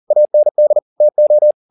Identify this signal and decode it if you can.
Morse code, heard as a keyed tone, about 40 wpm, 600 Hz. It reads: AND TO
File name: the old man and the sea 2 40wpm